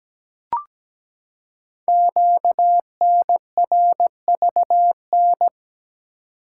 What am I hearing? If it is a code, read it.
Morse code, heard as a keyed tone, about 17 wpm, 700 Hz. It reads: QNRVN
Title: Code Groups - 5 Letters - 17wpm